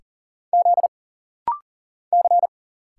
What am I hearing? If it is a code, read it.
Morse code, heard as a keyed tone, about 40 wpm, 700 Hz. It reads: Z C
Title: Single Letter-Number Alt - Rapid-Fire 40wpm